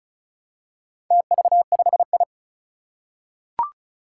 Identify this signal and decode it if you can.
Morse code, heard as a keyed tone, about 36 wpm, 700 Hz. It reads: TV5I
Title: Four Challenging Letter-Number Combinations 35wpm